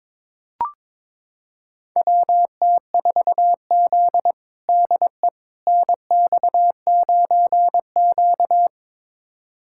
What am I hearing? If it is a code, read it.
Morse code, heard as a keyed tone, about 22 wpm, 700 Hz. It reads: WT4Z DE NX9Q
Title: USA Callsign DE Callsign - 4 Characters - 22wpm